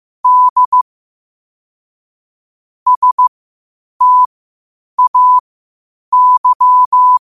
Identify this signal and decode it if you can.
Morse code, heard as a keyed tone, about 15 wpm, spaced as for 8 wpm, 1000 Hz. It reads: D STAY
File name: qotdfiles / 8wpm